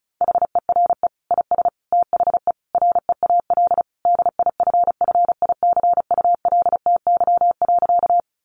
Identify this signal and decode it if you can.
Morse code, heard as a keyed tone, about 34 wpm, 700 Hz. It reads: HERE IS THE REAL DIFFICULTY.
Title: Sentences from Top 1500 Words 35wpm